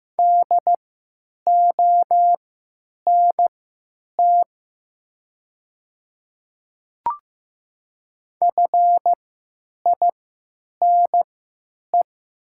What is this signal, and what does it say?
Morse code, heard as a keyed tone, about 15 wpm, spaced as for 9 wpm, 700 Hz. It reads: DONT FINE
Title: 3x Character Spacing - Top 500 Words - 15wpm